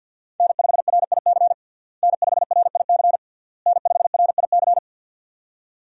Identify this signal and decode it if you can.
Morse code, heard as a keyed tone, about 51 wpm, 700 Hz. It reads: N5RIC N5RIC N5RIC
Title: USA Call Signs - 5 Characters - Repeated 3x - 50wpm